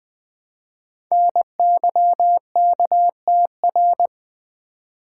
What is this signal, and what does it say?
Morse code, heard as a keyed tone, about 20 wpm, 700 Hz. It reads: NYKTR
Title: Code Groups - 5 Letters - 20wpm